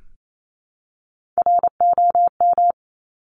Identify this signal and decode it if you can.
Morse code, heard as a keyed tone, about 28 wpm, 700 Hz. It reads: ROM
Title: Top Three Letter Combinations in Words 28wpm